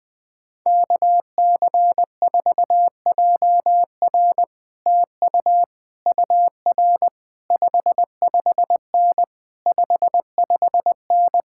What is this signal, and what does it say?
Morse code, heard as a keyed tone, about 20 wpm, 700 Hz. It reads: KC4JR TU UR 55N 55N
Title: Speed-Racing Common QSO Phrases - 17wpm